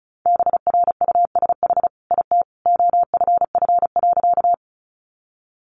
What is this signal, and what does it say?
Morse code, heard as a keyed tone, about 34 wpm, 700 Hz. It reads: BRUSH IT OFF.
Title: Sentences from Top 1100 Words 35wpm